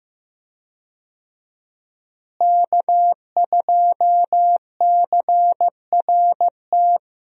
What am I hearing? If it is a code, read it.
Morse code, heard as a keyed tone, about 15 wpm, 700 Hz. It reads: K2CRT
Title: USA Call Signs - 5 Characters - Repeated 3x - 15wpm